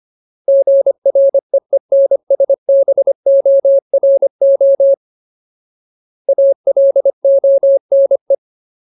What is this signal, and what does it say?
Morse code, heard as a keyed tone, about 25 wpm, 550 Hz. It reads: GREENSBORO ALONE
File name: Ultimate Word List / 25WPM Ultimate